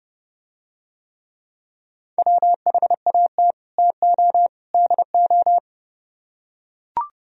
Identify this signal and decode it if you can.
Morse code, heard as a keyed tone, about 30 wpm, 700 Hz. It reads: WHAT TO DO